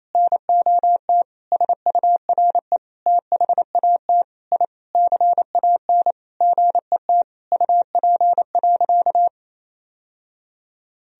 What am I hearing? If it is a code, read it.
Morse code, heard as a keyed tone, about 28 wpm, 700 Hz. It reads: NOT SURE THAT I CAN GET UP.
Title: Sentences from Top 1000 Words - Review Part 3 - 28wpm